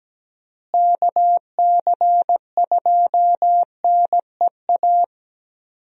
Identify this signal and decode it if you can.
Morse code, heard as a keyed tone, about 17 wpm, 700 Hz. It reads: KC2NEA
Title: USA Call Signs - 6 Characters - Repeated 2x - 17wpm